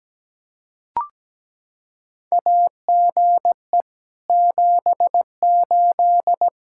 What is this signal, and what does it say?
Morse code, heard as a keyed tone, about 17 wpm, 700 Hz. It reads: AGE 78
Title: Practice QSO - Exchange 3 - More Info - 17wpm